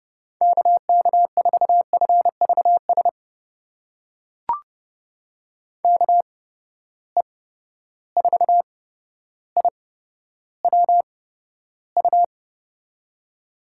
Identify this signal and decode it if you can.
Morse code, heard as a keyed tone, about 30 wpm, spaced as for 13 wpm, 700 Hz. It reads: KK4FVS KE4IWU
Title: USA Call Signs - 6 Characters - 8x Character Spacing - 30wpm